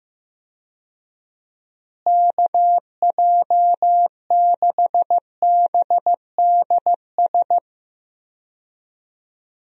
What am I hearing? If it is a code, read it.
Morse code, heard as a keyed tone, about 15 wpm, 700 Hz. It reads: KJ6BDS
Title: USA Call Signs - 6 Characters - Repeated 2x - 15wpm